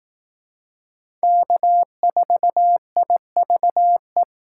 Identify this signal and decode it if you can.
Morse code, heard as a keyed tone, about 19 wpm, 700 Hz. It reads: K4IVE
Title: Speed Racing - USA 1x3 Call Signs 15wpm